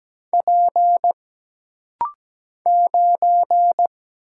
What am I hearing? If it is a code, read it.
Morse code, heard as a keyed tone, about 17 wpm, 700 Hz. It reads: P 9
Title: Single Letter-Number Alt - Rapid-Fire 17wpm